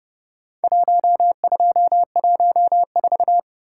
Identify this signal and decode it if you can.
Morse code, heard as a keyed tone, about 29 wpm, 700 Hz. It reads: 1214